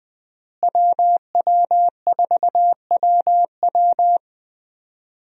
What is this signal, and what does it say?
Morse code, heard as a keyed tone, about 21 wpm, 700 Hz. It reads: WW4WW
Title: USA Call Signs - 5 Characters - Repeated 2x - 20wpm